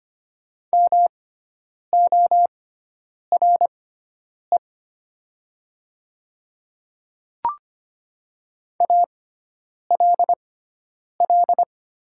Morse code, heard as a keyed tone, about 25 wpm, spaced as for 9 wpm, 690 Hz. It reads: MORE ALL
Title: 6x Character Spacing - Top 200 Words - 25wpm